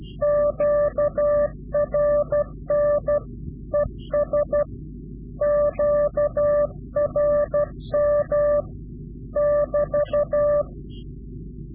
Morse code, heard as a keyed tone, about 12 wpm, 590 Hz. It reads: QRN ES QRM <BT>
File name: CW-QSO / Directional Call / Directional QSO